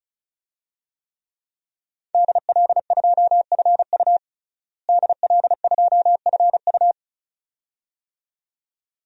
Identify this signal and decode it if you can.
Morse code, heard as a keyed tone, about 36 wpm, 700 Hz. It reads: DL2FU DL2FU
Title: DXCC-100 Call Signs - 5 Characters - Repeated 2x - 35wpm